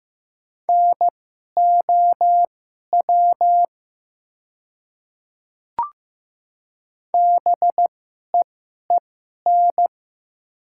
Morse code, heard as a keyed tone, about 15 wpm, spaced as for 11 wpm, 700 Hz. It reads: NOW BEEN
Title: 2x Character Spacing - Top 100 Words - 15wpm